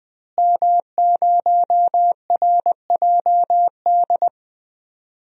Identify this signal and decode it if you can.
Morse code, heard as a keyed tone, about 20 wpm, 700 Hz. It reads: M0RJD
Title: DXCC-100 Call Signs - 5 Characters - Repeated 3x - 20wpm